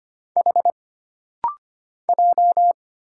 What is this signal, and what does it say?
Morse code, heard as a keyed tone, about 25 wpm, 700 Hz. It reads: H J
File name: Single Character (Letter:Number:Punctuation:Prosign) - Rapid-Fire - 25wpm